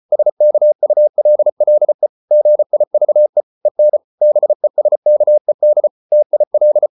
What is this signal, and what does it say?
Morse code, heard as a keyed tone, about 34 wpm, 600 Hz. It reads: SKULLE GIVE EN BESKED TIL
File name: Ferie QRQ 34wpm